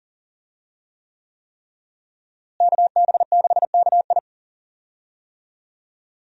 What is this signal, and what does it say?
Morse code, heard as a keyed tone, about 40 wpm, 700 Hz. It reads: KB6KI
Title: USA Call Signs - 5 Characters - Repeated 3x - 40wpm